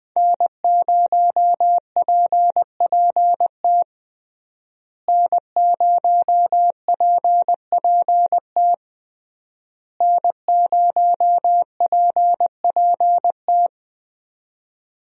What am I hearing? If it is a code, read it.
Morse code, heard as a keyed tone, about 20 wpm, 700 Hz. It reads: N0PPT N0PPT N0PPT
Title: USA Call Signs - 5 Characters - Repeated 3x - 20wpm